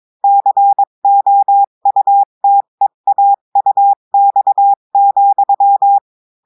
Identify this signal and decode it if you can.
Morse code, heard as a keyed tone, about 22 wpm, 800 Hz. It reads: COUTEAUX,
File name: cours-du-mercredi / audios / cours-853-du-2-mai-2024-etxte chiffre et texte clair-a-20-wpm